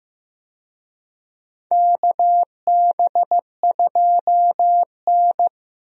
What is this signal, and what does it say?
Morse code, heard as a keyed tone, about 15 wpm, 700 Hz. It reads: KB2N
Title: USA Call Signs - 4 Characters - 15wpm